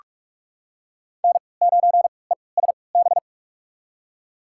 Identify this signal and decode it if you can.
Morse code, heard as a keyed tone, about 41 wpm, spaced as for 25 wpm, 700 Hz. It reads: N9ESB
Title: USA Call Signs - 5 Characters - 3x Character Spacing - 45wpm